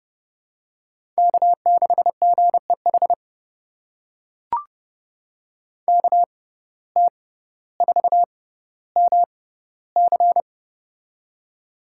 Morse code, heard as a keyed tone, about 30 wpm, spaced as for 15 wpm, 700 Hz. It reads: K6GEH KT4MC
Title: USA Call Signs - 5 Characters - 6x Character Spacing - 30wpm